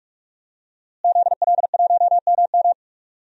Morse code, heard as a keyed tone, about 47 wpm, 700 Hz. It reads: ZL1MM
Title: DXCC-100 Call Signs - 5 Characters - Repeated 3x - 45wpm